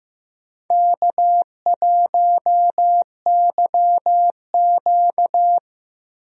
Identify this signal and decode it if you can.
Morse code, heard as a keyed tone, about 15 wpm, 700 Hz. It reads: K1YQ
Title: USA Call Signs - 4 Characters - 15wpm